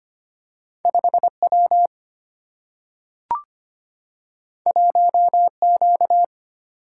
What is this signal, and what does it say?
Morse code, heard as a keyed tone, about 25 wpm, 700 Hz. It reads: 5W 1Q
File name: Two Letter-Number Combinations - 25wpm